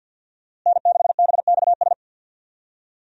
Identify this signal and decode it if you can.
Morse code, heard as a keyed tone, about 47 wpm, 700 Hz. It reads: N6BXS
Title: USA Call Signs - 5 Characters - Repeated 3x - 50wpm